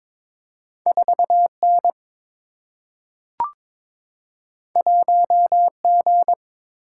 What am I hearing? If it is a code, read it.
Morse code, heard as a keyed tone, about 21 wpm, 700 Hz. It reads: 4N 1G
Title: Single-Letter Single-Number 22wpm